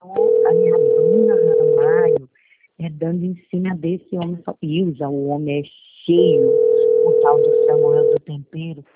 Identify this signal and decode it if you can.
Ringback tone (North America), 440 + 480 Hz, with speech.
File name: rg-1600-unknown- - - .4168